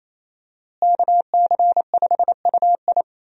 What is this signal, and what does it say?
Morse code, heard as a keyed tone, about 28 wpm, 700 Hz. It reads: KC5UI
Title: USA Call Signs - 5 Characters - Repeated 3x - 28wpm